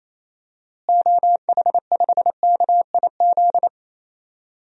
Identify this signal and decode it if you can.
Morse code, heard as a keyed tone, about 28 wpm, 700 Hz. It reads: OH5KIZ
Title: DXCC-100 Call Signs - 6 Characters - Repeated 3x - 28wpm